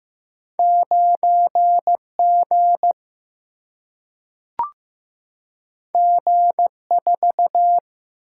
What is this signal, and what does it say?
Morse code, heard as a keyed tone, about 15 wpm, 700 Hz. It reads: 9 G G 4
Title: Single-Letter Single-Number 15wpm